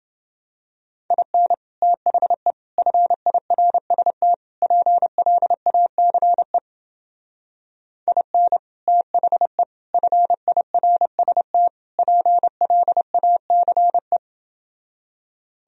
Morse code, heard as a keyed tone, about 29 wpm, 700 Hz. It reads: IN THE FIRST PLACE IN THE FIRST PLACE